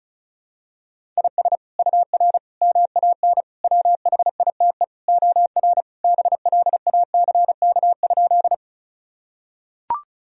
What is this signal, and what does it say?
Morse code, heard as a keyed tone, about 36 wpm, 700 Hz. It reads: IS UR MAN WHITE OR BLACK?
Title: Sentences from Top 400 Words and Abbreviations 35wpm